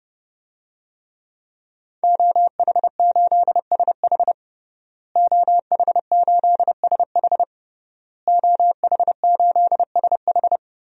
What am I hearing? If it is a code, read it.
Morse code, heard as a keyed tone, about 30 wpm, 700 Hz. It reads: OH8SH OH8SH OH8SH
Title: DXCC-100 Call Signs - 5 Characters - Repeated 3x - 30wpm